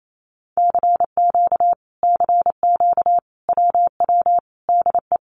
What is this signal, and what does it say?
Morse code, heard as a keyed tone, about 28 wpm, 700 Hz. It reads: CQ CQ WW DE
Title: Common QSO Phrases Encore 28wpm